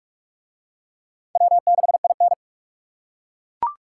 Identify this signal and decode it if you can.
Morse code, heard as a keyed tone, about 41 wpm, 700 Hz. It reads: W6IN